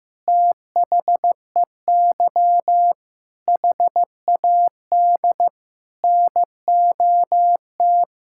Morse code, heard as a keyed tone, about 15 wpm, 700 Hz. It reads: THEY HAD NOT
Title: Sentences from Top 1600 Words 15wpm